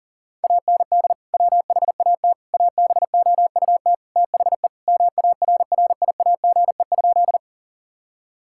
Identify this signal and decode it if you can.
Morse code, heard as a keyed tone, about 40 wpm, 700 Hz. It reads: AND WHAT ABOUT THE MARRIAGE?
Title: Sentences from Top 1000 Words - Review Part 5 - 40wpm